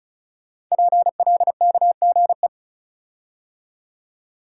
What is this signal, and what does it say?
Morse code, heard as a keyed tone, about 36 wpm, 700 Hz. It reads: PLKGE
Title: Code Groups - 5 Letters - 35wpm